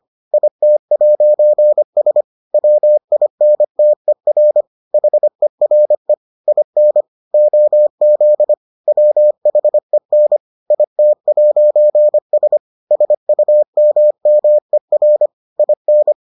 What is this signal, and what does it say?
Morse code, heard as a keyed tone, about 25 wpm, 600 Hz. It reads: IT'S WINTER HERE IN OZ WHEN IT'S SUMMER IN